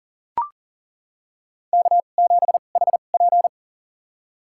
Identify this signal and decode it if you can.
Morse code, heard as a keyed tone, about 37 wpm, spaced as for 28 wpm, 700 Hz. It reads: K7HP
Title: USA Call Signs - 4 Characters - 2x Character Spacing - 40wpm